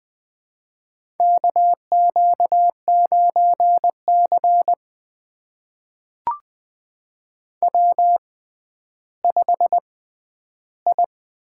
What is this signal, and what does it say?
Morse code, heard as a keyed tone, about 20 wpm, spaced as for 11 wpm, 700 Hz. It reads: KQ9C W5I
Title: USA Call Signs - 4 Characters - 6x Character Spacing - 20wpm